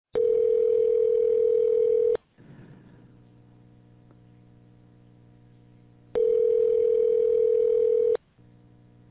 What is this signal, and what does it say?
Ringback tone (North America), 440 + 480 Hz.